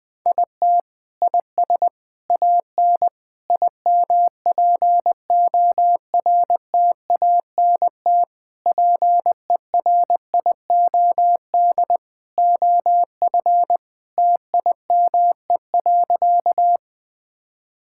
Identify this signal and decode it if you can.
Morse code, heard as a keyed tone, about 20 wpm, 700 Hz. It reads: IT IS AN IMPORTANT PERIOD OF TIME.
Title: Sentences from Top 1000 Words - Review Part 5 - 20wpm